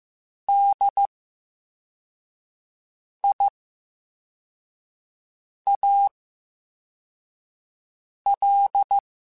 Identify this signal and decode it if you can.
Morse code, heard as a keyed tone, about 15 wpm, spaced as for 4 wpm, 800 Hz. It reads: DIAL
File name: Special INGs / Special INGs-mp3s / Slow ing @13